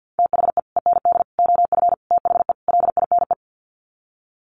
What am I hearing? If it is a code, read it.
Morse code, heard as a keyed tone, about 50 wpm, 700 Hz. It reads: THE END OF THE LINE
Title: Sets of 5 Words Encore2 50wpm